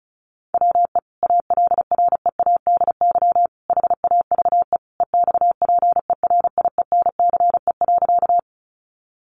Morse code, heard as a keyed tone, about 34 wpm, 700 Hz. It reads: WE ALREADY HAVE EXPERIENCE.
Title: Sentences from Top 700 Words 35wpm